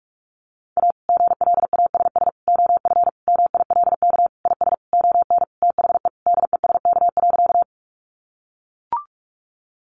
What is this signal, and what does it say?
Morse code, heard as a keyed tone, about 44 wpm, 700 Hz. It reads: A GLASS OF MILK IS ON THE DESK.
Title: Additional Easy Phrases 45wpm